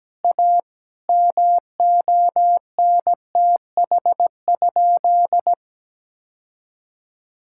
Morse code, heard as a keyed tone, about 17 wpm, 700 Hz. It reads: A MONTH?
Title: Sentences from Top 1000 Words - Review Part 5 - 17wpm